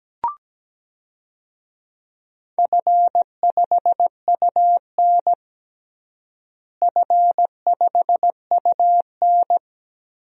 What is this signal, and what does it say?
Morse code, heard as a keyed tone, about 17 wpm, 700 Hz. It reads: F5UN F5UN
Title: DXCC-100 Call Signs - 4 Characters - Repeated 2x - 17wpm